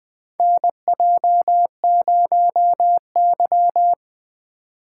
Morse code, heard as a keyed tone, about 20 wpm, 700 Hz. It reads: NJ0Y
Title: USA Call Signs - 4 Characters - 4x Character Spacing - 20wpm